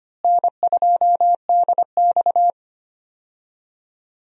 Morse code, heard as a keyed tone, about 25 wpm, 700 Hz. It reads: N2DX
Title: USA Call Signs - 4 Characters - Repeated 3x - 25wpm